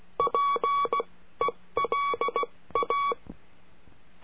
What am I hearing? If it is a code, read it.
Morse code, heard as a keyed tone, about 16 wpm, spaced as for 14 wpm, 1130 Hz. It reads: PELA
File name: CURSO DE TELEGRAFIA